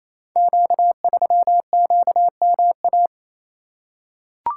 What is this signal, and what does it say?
Morse code, heard as a keyed tone, about 27 wpm, 690 Hz. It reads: Q3QMA
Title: Code Groups - 5 Letter:Numbers - 28wpm